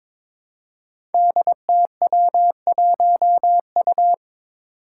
Morse code, heard as a keyed tone, about 23 wpm, 700 Hz. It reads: DTW1U
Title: Five Challenging Letter-Number Combinations 22wpm